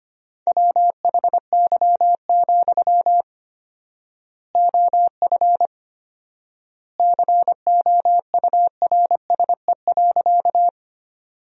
Morse code, heard as a keyed tone, about 25 wpm, 700 Hz. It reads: WHY, OF COURSE.
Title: 4x Word-Spacing - Sentences from Top 500 Words - 25wpm